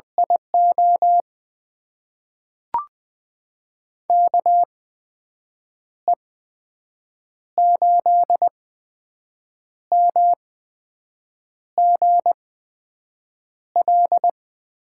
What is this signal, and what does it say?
Morse code, heard as a keyed tone, about 20 wpm, spaced as for 6 wpm, 700 Hz. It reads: IO KE8MGL
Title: USA Call Signs - 6 Characters - 8x Character Spacing - 20wpm